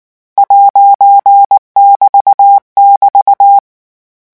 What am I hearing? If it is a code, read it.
Morse code, heard as a keyed tone, about 19 wpm, 800 Hz. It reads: '<BT><BT>
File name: morse mp3 / lecon14-vitesse 18